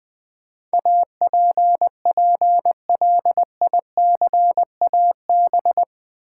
Morse code, heard as a keyed tone, about 20 wpm, 700 Hz. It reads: APPLICAB
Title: Build A Word - 4 Letters At A Time - 10-Letter Words - 20wpm